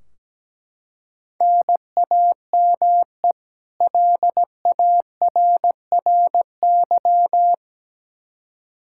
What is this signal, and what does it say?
Morse code, heard as a keyed tone, about 17 wpm, 700 Hz. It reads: NAME LARRY
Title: Practice QSO - Exchange 1 - Beginning - 17wpm